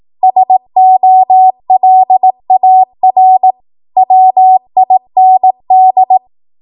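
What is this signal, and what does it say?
Morse code, heard as a keyed tone, about 18 wpm, 750 Hz. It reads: SOLAR WIND
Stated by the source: ARRL 18 wpm Morse code practice archive transmission.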